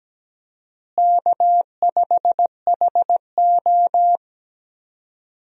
Morse code, heard as a keyed tone, about 17 wpm, 700 Hz. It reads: K5HO
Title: USA Call Signs - 4 Characters - Repeated 3x - 17wpm